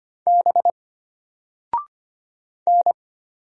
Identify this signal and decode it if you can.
Morse code, heard as a keyed tone, about 25 wpm, 700 Hz. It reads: B N
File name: Single-Letter-Number-25wpm